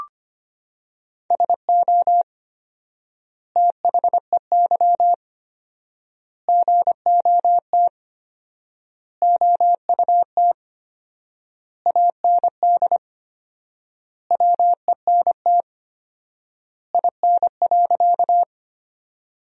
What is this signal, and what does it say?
Morse code, heard as a keyed tone, about 25 wpm, 700 Hz. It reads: SO THEY GOT OUT AND WENT IN.
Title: 4x Word-Spacing - Sentences from Top 300 Words - 25wpm